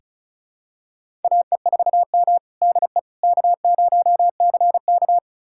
Morse code, heard as a keyed tone, about 36 wpm, 700 Hz. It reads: AE4M DE K0CK
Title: USA Callsign DE Callsign - 4 Characters - 35wpm